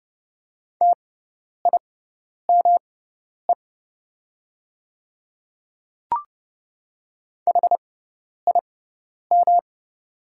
Morse code, heard as a keyed tone, about 32 wpm, spaced as for 10 wpm, 700 Hz. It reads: TIME HIM
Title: 6x Character Spacing - Top 100 Words - 30wpm